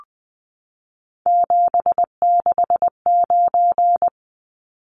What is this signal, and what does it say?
Morse code, heard as a keyed tone, about 19 wpm, 700 Hz. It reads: 769
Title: Numbers - 3-Digit 20wpm